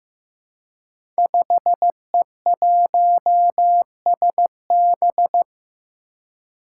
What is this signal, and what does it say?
Morse code, heard as a keyed tone, about 15 wpm, 700 Hz. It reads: 5E1SB
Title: Five Challenging Letter-Number Combinations 15wpm